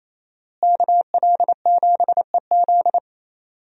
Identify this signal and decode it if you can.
Morse code, heard as a keyed tone, about 28 wpm, 700 Hz. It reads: KL7EZ
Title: DXCC-100 Call Signs - 5 Characters - Repeated 3x - 28wpm